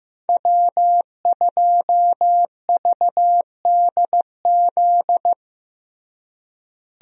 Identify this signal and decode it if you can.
Morse code, heard as a keyed tone, about 16 wpm, 700 Hz. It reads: W2VDZ
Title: USA Call Signs - 5 Characters - Repeated 2x - 15wpm